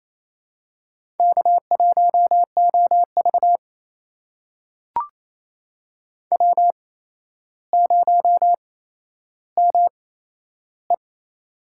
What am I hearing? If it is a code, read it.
Morse code, heard as a keyed tone, about 27 wpm, spaced as for 12 wpm, 700 Hz. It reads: K1OV W0ME
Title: USA Call Signs - 4 Characters - 8x Character Spacing - 28wpm